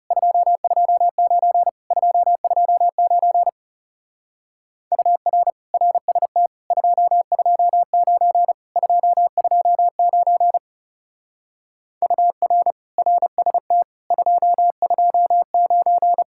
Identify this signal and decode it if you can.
Morse code, heard as a keyed tone, about 37 wpm, 700 Hz. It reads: 229 229 UR RST 229 229 UR RST 229
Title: Speed-Racing Common QSO Phrases Encore - 30wpm